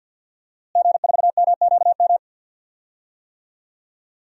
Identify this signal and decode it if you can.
Morse code, heard as a keyed tone, about 52 wpm, 700 Hz. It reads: G4MQM
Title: DXCC-100 Call Signs - 5 Characters - Repeated 3x - 50wpm